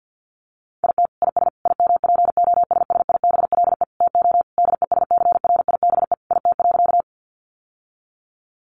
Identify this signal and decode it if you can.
Morse code, heard as a keyed tone, about 50 wpm, 700 Hz. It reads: IT IS IMPOSSIBLE TO DESCRIBE IT.
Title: Sentences from Top 1100 Words 50wpm